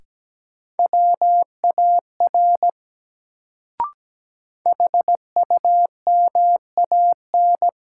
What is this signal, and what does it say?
Morse code, heard as a keyed tone, about 17 wpm, 700 Hz. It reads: WAR HUMAN
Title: Sets of 5 Words - New Word Drill - 17wpm